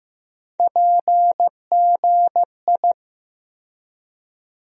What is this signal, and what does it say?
Morse code, heard as a keyed tone, about 15 wpm, 700 Hz. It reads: PGI
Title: USA Call Sign Postfixes - 3 Letters 15wpm